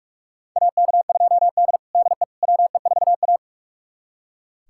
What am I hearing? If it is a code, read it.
Morse code, heard as a keyed tone, about 44 wpm, 700 Hz. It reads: AK2D DE WE4A
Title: USA Callsign DE Callsign - 4 Characters - 45wpm